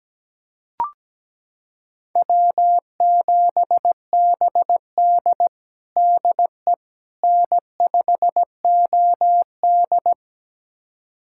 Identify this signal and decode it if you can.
Morse code, heard as a keyed tone, about 17 wpm, 700 Hz. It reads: W7BD DE N5OD
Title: USA Callsign DE Callsign - 4 Characters - 17wpm